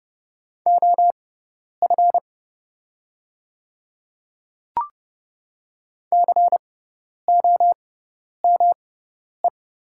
Morse code, heard as a keyed tone, about 30 wpm, spaced as for 10 wpm, 700 Hz. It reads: OF COME
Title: 6x Character Spacing - Top 100 Words - 30wpm